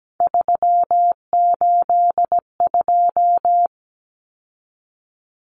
Morse code, heard as a keyed tone, about 17 wpm, 700 Hz. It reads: 382
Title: Common Cut Numbers - 3-Digit 17wpm